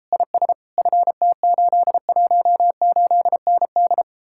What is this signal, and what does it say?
Morse code, heard as a keyed tone, about 33 wpm, 700 Hz. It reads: IS FT818ND